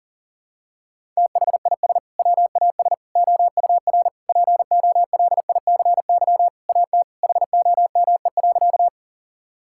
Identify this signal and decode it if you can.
Morse code, heard as a keyed tone, about 40 wpm, 700 Hz. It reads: THIS WAS OUR POLICY AT HOME.
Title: Sentences from Top 1000 Words - Review Part 5 - 40wpm